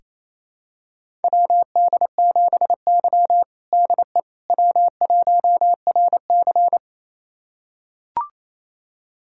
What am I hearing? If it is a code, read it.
Morse code, heard as a keyed tone, about 28 wpm, 700 Hz. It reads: WD7Y DE W1RC
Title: USA Callsign DE Callsign - 4 Characters - 28wpm